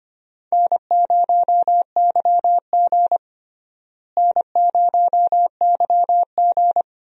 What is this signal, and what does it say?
Morse code, heard as a keyed tone, about 25 wpm, 700 Hz. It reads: N0YG N0YG
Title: USA Call Signs - 4 Characters - Repeated 2x - 25wpm